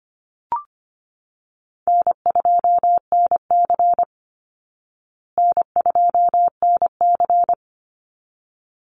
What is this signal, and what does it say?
Morse code, heard as a keyed tone, about 25 wpm, 700 Hz. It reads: N2NC N2NC
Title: Call Signs Repeated 3x - 25wpm